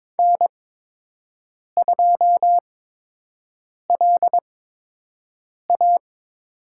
Morse code, heard as a keyed tone, about 21 wpm, spaced as for 6 wpm, 700 Hz. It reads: N2LA
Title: USA Call Signs - 8x Character Spacing - 22wpm